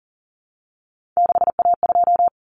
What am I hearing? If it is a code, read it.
Morse code, heard as a keyed tone, about 40 wpm, 700 Hz. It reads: 6A2
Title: Common Cut Numbers - 3-Digit 40wpm